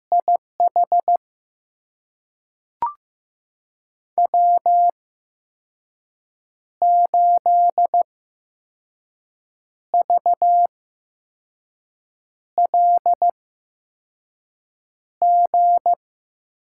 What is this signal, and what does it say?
Morse code, heard as a keyed tone, about 15 wpm, spaced as for 5 wpm, 700 Hz. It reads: IH W8VLG
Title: USA Call Signs - 8x Character Spacing - 15wpm